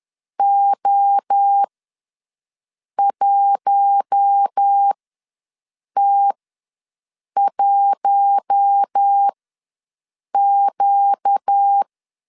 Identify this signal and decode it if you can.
Morse code, heard as a keyed tone, about 10 wpm, spaced as for 6 wpm, 800 Hz. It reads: O1T1Q
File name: LEZIONI / 7 GF9Z / 7 GF9Z ws 10wpm 3919